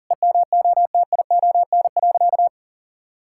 Morse code, heard as a keyed tone, about 40 wpm, 700 Hz. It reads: EMOTION.
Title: Sentences from Top 1600 Words 40wpm